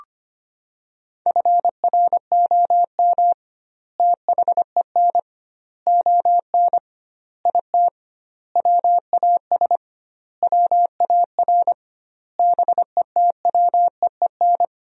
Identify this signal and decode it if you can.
Morse code, heard as a keyed tone, about 25 wpm, 700 Hz. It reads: FROM THEN ON IT WAS WAR BETWEEN
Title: 2x Word-Spacing - Sentences from Top 400 Words - 25wpm